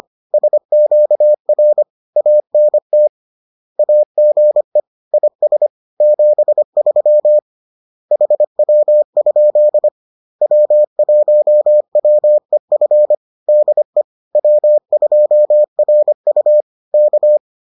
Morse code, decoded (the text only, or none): SQR ANT AGE IS 73 HW? W1WEF DE W2RU K